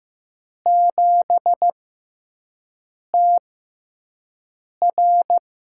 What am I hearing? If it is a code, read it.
Morse code, heard as a keyed tone, about 15 wpm, spaced as for 5 wpm, 700 Hz. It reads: 7TR
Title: USA Call Signs - 6 Characters - 6x Character Spacing - 15wpm